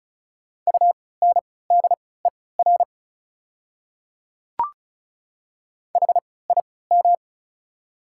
Morse code, heard as a keyed tone, about 36 wpm, spaced as for 20 wpm, 700 Hz. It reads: UNDER HIM
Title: 3x Character Spacing - Top 200 Words - 35wpm